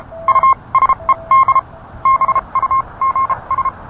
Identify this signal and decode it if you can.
Morse code, heard as a keyed tone, about 31 wpm, spaced as for 25 wpm, 1060 Hz. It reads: USED BUGS